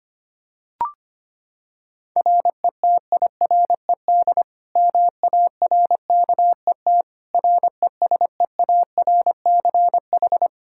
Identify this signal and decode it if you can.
Morse code, heard as a keyed tone, about 25 wpm, 700 Hz. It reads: RETIRED MARKET RESEARCH
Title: Practice QSO - Exchange 3 - More Info - 25wpm